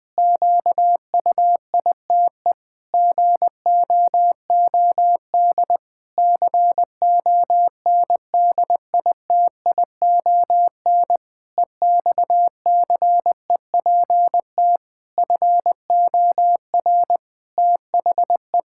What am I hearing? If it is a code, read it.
Morse code, heard as a keyed tone, about 20 wpm, 700 Hz. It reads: QUITE GOOD CONDITION EXCEPT FOR THE
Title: Pangram Challenge v2 - 20wpm